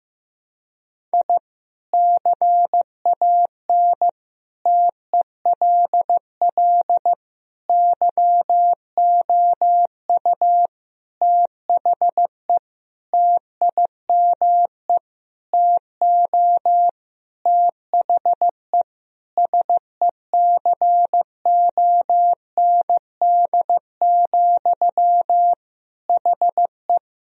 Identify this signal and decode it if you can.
Morse code, heard as a keyed tone, about 15 wpm, 700 Hz. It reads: I CAN TELL YOU THE TIME TO THE SECOND, HE
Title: Sentences from Top 1000 Words - Review Part 1 - 15wpm